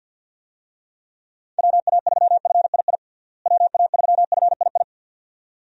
Morse code, heard as a keyed tone, about 50 wpm, 700 Hz. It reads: WA3FII WA3FII
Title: USA Call Signs - 6 Characters - Repeated 2x - 50wpm